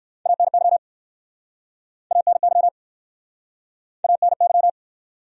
Morse code, heard as a keyed tone, about 58 wpm, 700 Hz. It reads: ANY ANY ANY